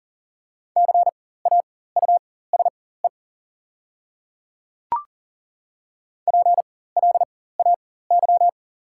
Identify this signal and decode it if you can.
Morse code, heard as a keyed tone, about 40 wpm, spaced as for 19 wpm, 690 Hz. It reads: CAUSE PLAY
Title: 4x Character Spacing - Top 500 Words - 40wpm